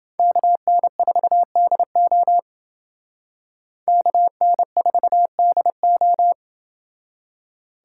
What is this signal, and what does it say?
Morse code, heard as a keyed tone, about 29 wpm, 700 Hz. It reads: KN4DO KN4DO